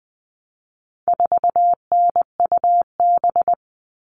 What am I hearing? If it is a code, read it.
Morse code, heard as a keyed tone, about 20 wpm, 700 Hz. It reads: 4NUB
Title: All Cut Numbers - 4-Digit 20wpm